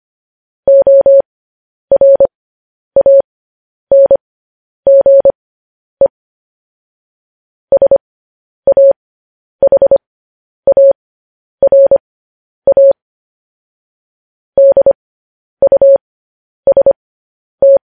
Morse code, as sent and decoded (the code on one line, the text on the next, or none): --- .-. .- -. --. / ... .- .... .- .-. .- / -.. ..- ... -
ORANGE SAHARA DUST